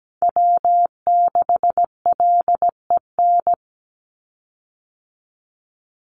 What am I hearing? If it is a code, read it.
Morse code, heard as a keyed tone, about 17 wpm, 700 Hz. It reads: W6LEN
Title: Call Signs Repeated 3x - 17wpm